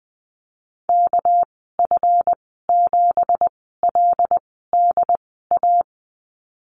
Morse code, heard as a keyed tone, about 20 wpm, 700 Hz. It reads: K F 7 L D A
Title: 207 Lesson 33g - Letter j - Call Signs 20wpm